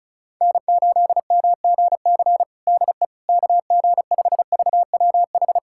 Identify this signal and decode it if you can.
Morse code, heard as a keyed tone, about 36 wpm, 700 Hz. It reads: N8MGC DE KG5VWH
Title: USA Callsign DE Callsign 35wpm